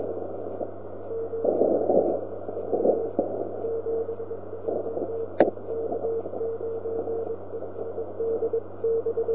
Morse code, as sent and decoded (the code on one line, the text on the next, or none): none